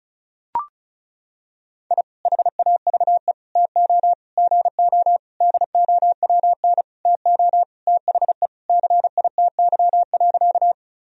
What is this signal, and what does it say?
Morse code, heard as a keyed tone, about 36 wpm, 700 Hz. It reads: I HAVE TO GO DOWN TO THE CITY.
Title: Sentences from Top 1000 Words - Review Part 1 - 35wpm